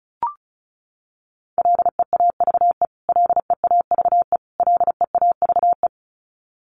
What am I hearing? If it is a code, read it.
Morse code, heard as a keyed tone, about 34 wpm, 700 Hz. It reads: LEAVE LEAVE LEAVE
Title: Top 500 Words Repeated 35wpm